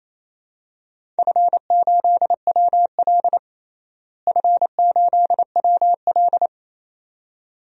Morse code, heard as a keyed tone, about 28 wpm, 700 Hz. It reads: F8WL F8WL
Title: DXCC-100 Call Signs - 4 Characters - Repeated 2x - 28wpm